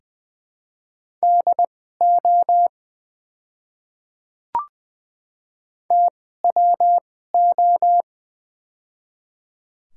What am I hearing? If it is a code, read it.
Morse code, heard as a keyed tone, about 20 wpm, spaced as for 14 wpm, 700 Hz. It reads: DO TWO
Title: 2x Character Spacing - Top 200 Words - 20wpm